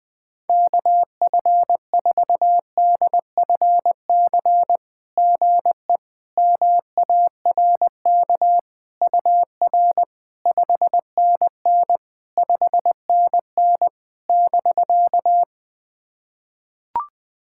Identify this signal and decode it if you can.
Morse code, heard as a keyed tone, about 20 wpm, 700 Hz. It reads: KF4DFC GE MARK UR 5NN 5NN <BK>
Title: SOTA QSO Practice 20wpm